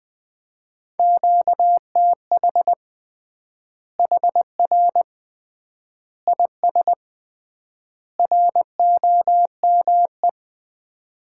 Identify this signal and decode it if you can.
Morse code, heard as a keyed tone, about 20 wpm, 700 Hz. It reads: QTH HR IS ROME
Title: Common QSO Phrases - 3x Word-Spacing - 20wpm